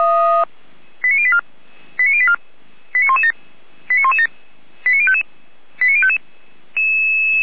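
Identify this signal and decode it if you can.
Five-tone selective call (ZVEI), heard as quick runs of five tones, tones of about 70 ms. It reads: 89084 89084 89117 89117 89055 89055
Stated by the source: Five-tone ZVEI selective call received by SDR.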